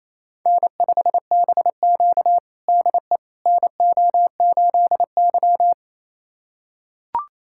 Morse code, heard as a keyed tone, about 28 wpm, 700 Hz. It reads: N5BQ DE NO8Y
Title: USA Callsign DE Callsign - 4 Characters - 28wpm